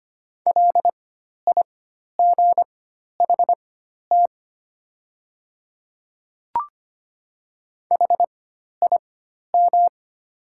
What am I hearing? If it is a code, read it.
Morse code, heard as a keyed tone, about 25 wpm, spaced as for 12 wpm, 690 Hz. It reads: LIGHT HIM
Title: 4x Character Spacing - Top 200 Words - 25wpm